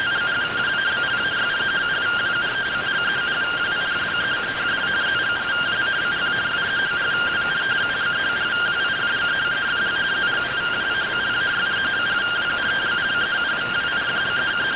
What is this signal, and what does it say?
Two-tone teleprinter signal, about 170 Hz shift.